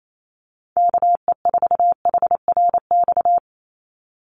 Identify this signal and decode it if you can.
Morse code, heard as a keyed tone, about 28 wpm, 700 Hz. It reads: KE4HRX
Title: USA Call Signs 28wpm